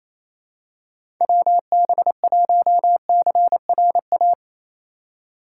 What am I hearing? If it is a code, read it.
Morse code, heard as a keyed tone, about 27 wpm, 700 Hz. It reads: WB1CRA
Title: USA Call Signs - 6 Characters - 8x Character Spacing - 28wpm